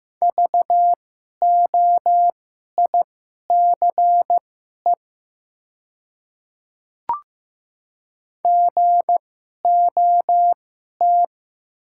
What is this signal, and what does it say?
Morse code, heard as a keyed tone, about 15 wpm, spaced as for 11 wpm, 700 Hz. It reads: VOICE GOT
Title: 2x Character Spacing - Top 500 Words - 15wpm